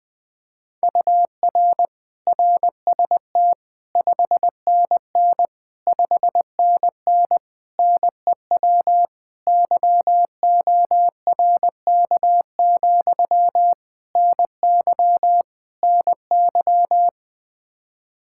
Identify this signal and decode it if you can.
Morse code, heard as a keyed tone, about 21 wpm, 700 Hz. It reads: UR RST 5NN 5NN NEW YORK, NY NY
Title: Practice QSO - Exchange 1 - Beginning - 20wpm